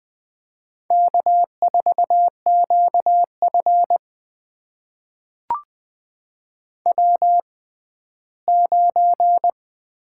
Morse code, heard as a keyed tone, about 20 wpm, spaced as for 14 wpm, 700 Hz. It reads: K4QF W9
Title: USA Call Signs - 4 Characters - 6x Character Spacing - 20wpm